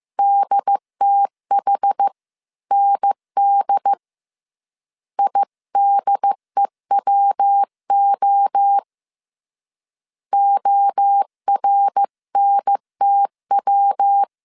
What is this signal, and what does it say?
Morse code, heard as a keyed tone, about 15 wpm, 800 Hz. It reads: DTHND IDEWO ORNTW